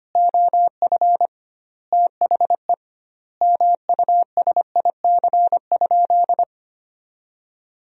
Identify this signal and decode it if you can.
Morse code, heard as a keyed tone, about 25 wpm, 700 Hz. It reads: OF THE MUSIC?
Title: 2x Word-Spacing - Sentences from Top 300 Words - 25wpm